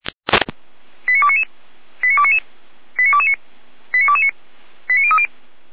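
Five-tone selective call (ZVEI), heard as quick runs of five tones, tones of about 70 ms. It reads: 89299 89299 89229 89229 89030
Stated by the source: ZVEI five-tone selective call received via SDR, from the Feuerwehr Orlamünde audio server.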